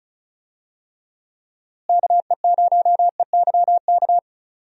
Morse code, heard as a keyed tone, about 36 wpm, 700 Hz. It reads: KE0EYK
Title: USA Call Signs - 6 Characters - Repeated 3x - 35wpm